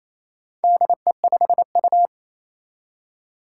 Morse code, heard as a keyed tone, about 28 wpm, 700 Hz. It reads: DE5U
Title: Four Challenging Letter-Number Combinations 28wpm